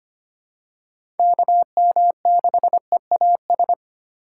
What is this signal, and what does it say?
Morse code, heard as a keyed tone, about 25 wpm, 700 Hz. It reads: KM6EAS
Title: USA Call Signs - 6x Character Spacing - 25wpm